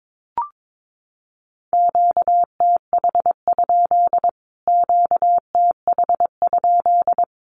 Morse code, heard as a keyed tone, about 22 wpm, 700 Hz. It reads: QTH? QTH?
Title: Q-Codes Repeated 3x 22wpm